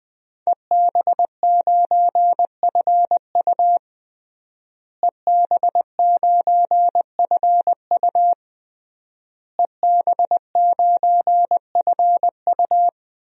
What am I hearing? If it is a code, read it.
Morse code, heard as a keyed tone, about 20 wpm, 700 Hz. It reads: EB9FU EB9FU EB9FU
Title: DXCC-100 Call Signs - 5 Characters - Repeated 3x - 20wpm